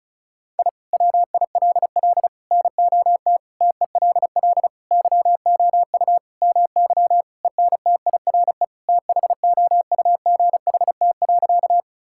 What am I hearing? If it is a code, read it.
Morse code, heard as a keyed tone, about 34 wpm, 700 Hz. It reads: I WILL NOT TELL YOU MY ENTIRE THOUGHT.
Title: Sentences from Top 1000 Words - Review Part 5 - 35wpm